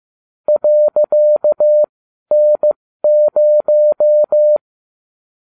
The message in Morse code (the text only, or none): .N0